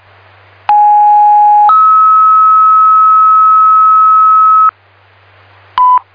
Two-tone page, heard as a one-second tone, then a three-second tone, about 830 then 1290 Hz. It is heard from 0.7 to 4.7 s.